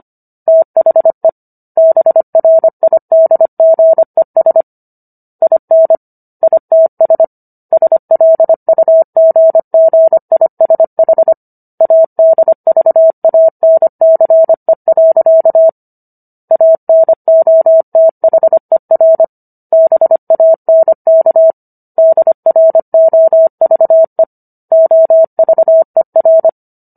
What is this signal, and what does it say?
Morse code, heard as a keyed tone, about 25 wpm, 650 Hz. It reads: THE BRIDGES IN ITS SLUGGISH ADVANCE. ANOTHER BANK DROVE OVER